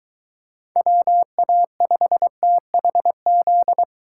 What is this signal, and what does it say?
Morse code, heard as a keyed tone, about 21 wpm, 700 Hz. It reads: WA5THZ